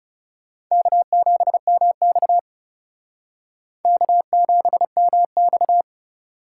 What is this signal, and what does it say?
Morse code, heard as a keyed tone, about 37 wpm, 700 Hz. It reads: K7MX K7MX
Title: Speed Racing - USA 1x2 Call Signs 30wpm